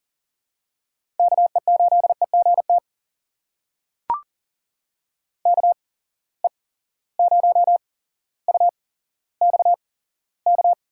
Morse code, heard as a keyed tone, about 42 wpm, spaced as for 17 wpm, 700 Hz. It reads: KE8EGT KE0UXK
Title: USA Call Signs - 6 Characters - 8x Character Spacing - 40wpm